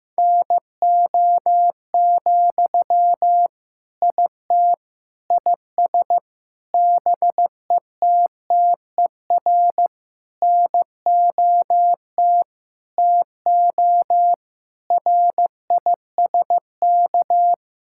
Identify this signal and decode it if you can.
Morse code, heard as a keyed tone, about 15 wpm, 700 Hz. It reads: NO, IT IS BETTER NOT TO RISK
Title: Sentences from Top 1600 Words 15wpm